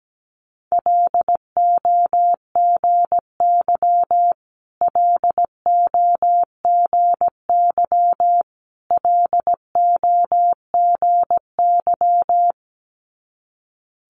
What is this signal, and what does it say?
Morse code, heard as a keyed tone, about 17 wpm, 700 Hz. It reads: LOGY LOGY LOGY
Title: Multi-letter Phonemes Repeated 3x 17wpm